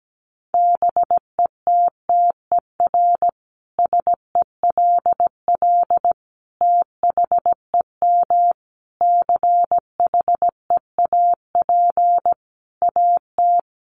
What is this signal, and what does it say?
Morse code, heard as a keyed tone, about 17 wpm, 700 Hz. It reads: BETTER SELL THEM CHEAP AT
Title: Sentences from Top 1200 Words 17wpm